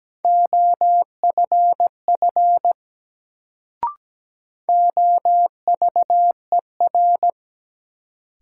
Morse code, heard as a keyed tone, about 17 wpm, 700 Hz. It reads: OFF OVER
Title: Sets of 2 Words - New Word Drill - 17wpm